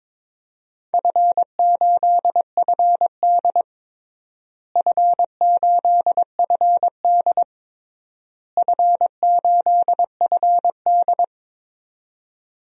Morse code, heard as a keyed tone, about 22 wpm, 700 Hz. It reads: F8FD F8FD F8FD
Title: DXCC-100 Call Signs - 4 Characters - Repeated 3x - 22wpm